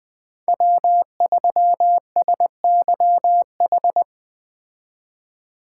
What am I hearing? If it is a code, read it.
Morse code, heard as a keyed tone, about 20 wpm, 700 Hz. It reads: W3SYH
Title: Speed Racing - USA 1x3 Call Signs 17wpm